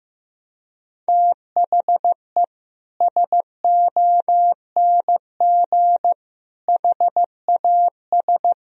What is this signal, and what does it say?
Morse code, heard as a keyed tone, about 15 wpm, 700 Hz. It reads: THE SONG HAS
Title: Sentences from Top 1000 Words - Review Part 3 - 15wpm